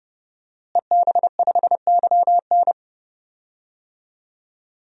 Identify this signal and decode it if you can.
Morse code, heard as a keyed tone, about 31 wpm, 700 Hz. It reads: EB5YN
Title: DXCC-100 Call Signs - 5 Characters - Repeated 2x - 30wpm